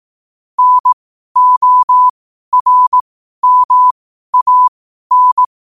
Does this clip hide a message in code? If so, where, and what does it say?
Yes: Morse code from 0.6 to 5.5 s, heard as a keyed tone, about 18 wpm, spaced as for 12 wpm, 1000 Hz. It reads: NORMAN